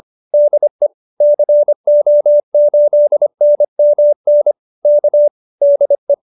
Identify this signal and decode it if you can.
Morse code, heard as a keyed tone, about 25 wpm, 570 Hz. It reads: DE CO8NMN K DE
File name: QSO304 25